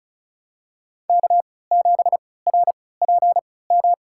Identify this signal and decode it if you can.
Morse code, heard as a keyed tone, about 36 wpm, spaced as for 20 wpm, 700 Hz. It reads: K7RPM
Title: USA Call Signs - 5 Characters - 3x Character Spacing - 35wpm